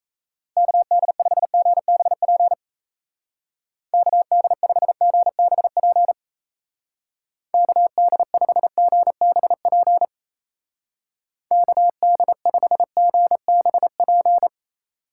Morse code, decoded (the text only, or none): KD5GBP KD5GBP KD5GBP KD5GBP